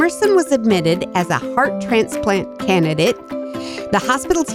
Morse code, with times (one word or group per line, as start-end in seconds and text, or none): none